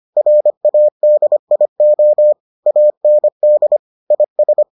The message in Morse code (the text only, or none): RADIO AND IS